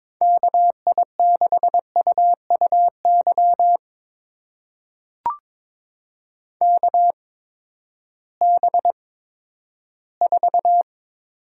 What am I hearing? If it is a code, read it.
Morse code, heard as a keyed tone, about 22 wpm, spaced as for 13 wpm, 700 Hz. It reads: KI6UUY KB4
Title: USA Call Signs - 6 Characters - 8x Character Spacing - 22wpm